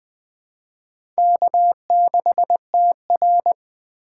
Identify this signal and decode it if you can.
Morse code, heard as a keyed tone, about 20 wpm, 700 Hz. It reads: K6TR
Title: USA Call Signs - 4 Characters - 4x Character Spacing - 20wpm